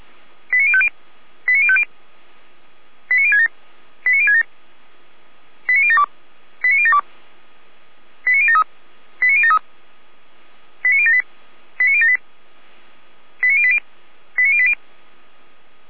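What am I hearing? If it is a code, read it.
Five-tone selective call (ZVEI), heard as quick runs of five tones, tones of about 70 ms. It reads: 89050 89050 89067 89067 89072 89072 89073 89073 89078 89078 89080 89080
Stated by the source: Five-tone ZVEI selective call received on an SDR, address 89014, channel 455.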